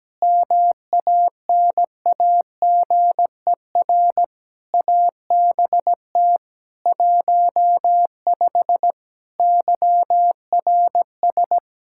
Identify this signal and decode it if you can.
Morse code, heard as a keyed tone, about 17 wpm, 700 Hz. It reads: MANAGER ABT 15 YRS
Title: Practice QSO - Exchange 3 - More Info - 17wpm